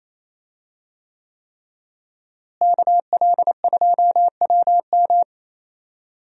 Morse code, heard as a keyed tone, about 27 wpm, 700 Hz. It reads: KL2WM
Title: USA Call Signs - 5 Characters - Repeated 3x - 28wpm